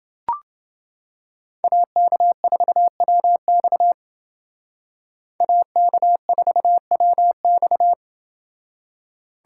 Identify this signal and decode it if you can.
Morse code, heard as a keyed tone, about 29 wpm, 700 Hz. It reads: AK4WX AK4WX